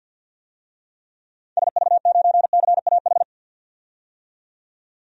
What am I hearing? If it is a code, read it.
Morse code, heard as a keyed tone, about 52 wpm, 700 Hz. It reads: SV9CAH
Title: DXCC-100 Call Signs - 6 Characters - 50wpm